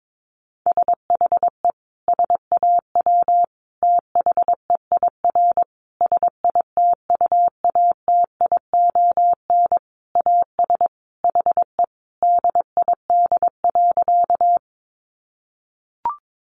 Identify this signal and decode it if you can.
Morse code, heard as a keyed tone, about 22 wpm, 700 Hz. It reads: SHE SAW THEIR SITUATION AS HE DID.
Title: Sentences from Top 1200 Words 22wpm